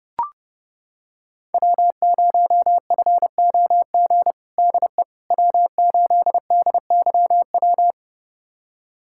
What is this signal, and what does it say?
Morse code, heard as a keyed tone, about 30 wpm, 700 Hz. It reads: W0FOG DE W8DYW
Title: USA Callsign DE Callsign - 5 Characters - 30wpm